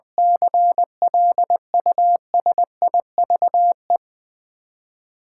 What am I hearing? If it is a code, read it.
Morse code, heard as a keyed tone, about 21 wpm, 700 Hz. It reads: CLUSIVE
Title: Nine Letter Words - 20wpm